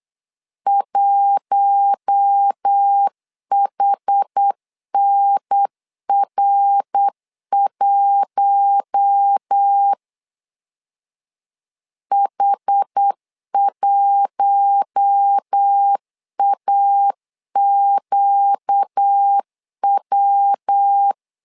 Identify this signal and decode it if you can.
Morse code, heard as a keyed tone, about 8 wpm, 800 Hz. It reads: 1HNR1 H1AQW
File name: LEZIONI / 4 Q1 8wpm 3035